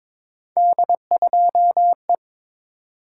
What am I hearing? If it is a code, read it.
Morse code, heard as a keyed tone, about 22 wpm, 700 Hz. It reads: D2E
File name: practice / 22wpm / Three Letter-Number Combinations - 22wpm